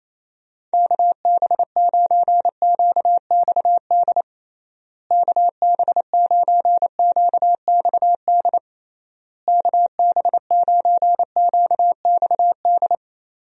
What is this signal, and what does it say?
Morse code, heard as a keyed tone, about 28 wpm, 700 Hz. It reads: KB9QXD KB9QXD KB9QXD
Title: USA Call Signs - 6 Characters - Repeated 3x - 28wpm